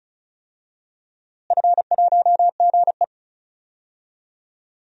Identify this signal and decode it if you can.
Morse code, heard as a keyed tone, about 36 wpm, 700 Hz. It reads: F1GE
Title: DXCC-100 Call Signs - 4 Characters - Repeated 3x - 35wpm